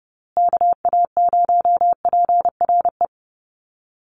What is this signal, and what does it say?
Morse code, heard as a keyed tone, about 30 wpm, 700 Hz. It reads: KA0PRE
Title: USA Call Signs Repeated 3x 30wpm